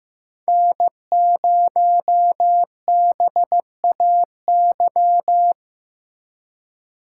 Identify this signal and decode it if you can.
Morse code, heard as a keyed tone, about 15 wpm, 700 Hz. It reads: N0BAY
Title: USA Call Signs - 5 Characters - 2x Character Spacing - 15wpm